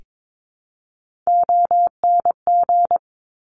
Morse code, heard as a keyed tone, about 22 wpm, 700 Hz. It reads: ONG
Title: Top Three Letter Combinations in Words 22wpm